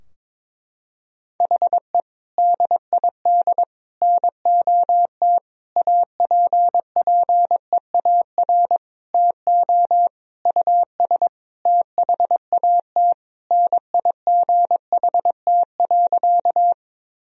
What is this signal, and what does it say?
Morse code, heard as a keyed tone, about 22 wpm, 700 Hz. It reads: HE DID NOT APPEAR TO US THAT NIGHT.
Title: Sentences from Top 1000 Words - Review Part 2 - 22wpm